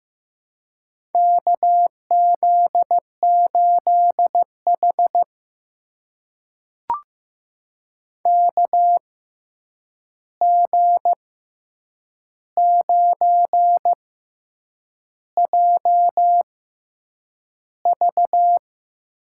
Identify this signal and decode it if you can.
Morse code, heard as a keyed tone, about 16 wpm, spaced as for 7 wpm, 700 Hz. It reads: KZ8H KG9JV
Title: USA Call Signs - 6x Character Spacing - 15wpm